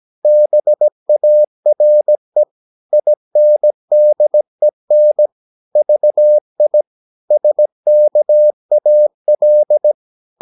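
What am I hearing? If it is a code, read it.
Morse code, heard as a keyed tone, about 17 wpm, 600 Hz. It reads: BARE INDEN VI SKAL
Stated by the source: Slow (QRS) Morse practice at 17 wpm, from CW med Gnister.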